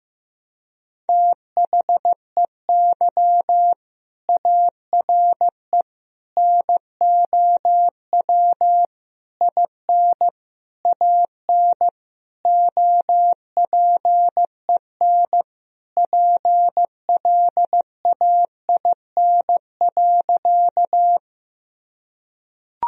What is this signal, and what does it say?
Morse code, heard as a keyed tone, about 15 wpm, 700 Hz. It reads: THEY ARE NOW IN AN OPEN PLAIN.
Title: Sentences from Top 1000 Words - Review Part 2 - 15wpm